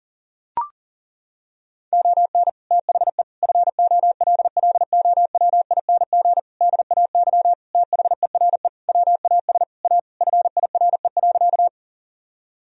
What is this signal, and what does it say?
Morse code, heard as a keyed tone, about 40 wpm, 700 Hz. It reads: ON THE FOLLOWING DAY THERE WAS A FIRE.
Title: Sentences from Top 1000 Words - Review Part 4 - 40wpm